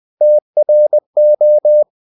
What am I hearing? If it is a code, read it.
Morse code, heard as a keyed tone, about 20 wpm, 600 Hz. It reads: TRO
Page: Ferie 20 wpm